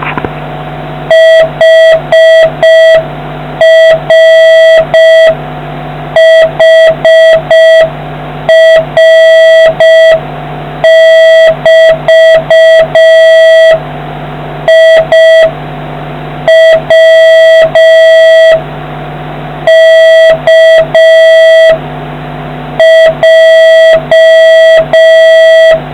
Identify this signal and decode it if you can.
Morse code, heard as a keyed tone, about 5 wpm, 1950 Hz, with noise. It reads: HRHR<BT>I W K J